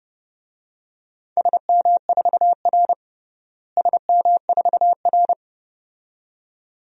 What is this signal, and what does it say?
Morse code, heard as a keyed tone, about 30 wpm, 700 Hz. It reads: SM4R SM4R
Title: DXCC-100 Call Signs - 4 Characters - Repeated 2x - 30wpm